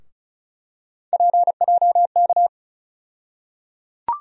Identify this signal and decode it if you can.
Morse code, heard as a keyed tone, about 36 wpm, 700 Hz. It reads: PJK
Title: USA Call Sign Postfixes - 3 Letters 35wpm